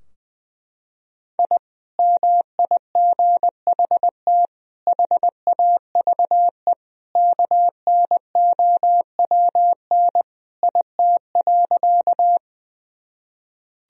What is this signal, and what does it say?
Morse code, heard as a keyed tone, about 20 wpm, 700 Hz. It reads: I MIGHT HAVE KNOWN IT.
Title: Sentences from Top 1000 Words - Review Part 3 - 20wpm